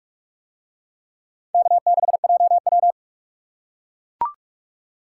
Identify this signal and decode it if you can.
Morse code, heard as a keyed tone, about 44 wpm, 700 Hz. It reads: K6JW